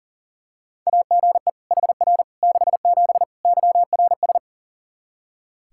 Morse code, heard as a keyed tone, about 40 wpm, 700 Hz. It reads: AGE HR 67 YRS
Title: Practice QSO - Exchange 3 - More Info - 40wpm